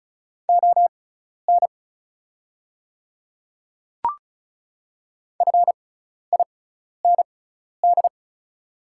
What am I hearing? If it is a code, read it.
Morse code, heard as a keyed tone, about 34 wpm, spaced as for 12 wpm, 700 Hz. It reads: ON FIND
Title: 6x Character Spacing - Top 100 Words - 35wpm